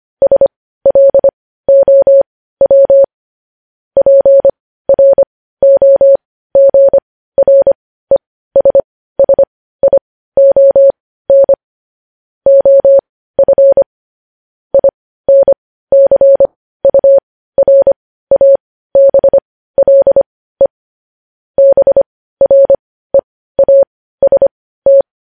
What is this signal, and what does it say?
Morse code, heard as a keyed tone, about 25 wpm, spaced as for 15 wpm, 550 Hz. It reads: SLOW PROGRESSION OF INCURABLE BREAST
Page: news-headlines-in-morse-code-at-15-wpm